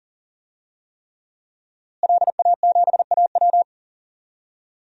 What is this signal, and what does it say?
Morse code, heard as a keyed tone, about 40 wpm, 700 Hz. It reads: LA7AW
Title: DXCC-100 Call Signs - 5 Characters - Repeated 3x - 40wpm